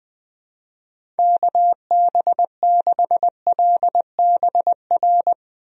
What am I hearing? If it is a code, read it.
Morse code, heard as a keyed tone, about 20 wpm, 700 Hz. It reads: KB6LBR
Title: USA Call Signs - 6 Characters - Repeated 2x - 20wpm